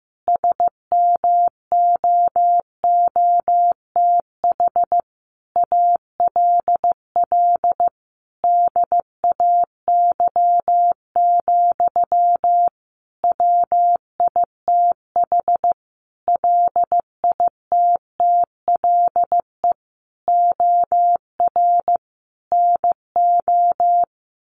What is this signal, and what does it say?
Morse code, heard as a keyed tone, about 15 wpm, 700 Hz. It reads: SMOOTH ALL DAY, WITH LITTLE OR NO
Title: Sentences from Top 1400 Words 15wpm